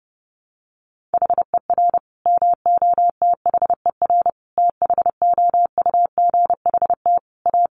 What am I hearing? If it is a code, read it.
Morse code, heard as a keyed tone, about 28 wpm, 700 Hz. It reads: HER MOTHER THOUGHT A
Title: Sentences from Top 800 Words 30wpm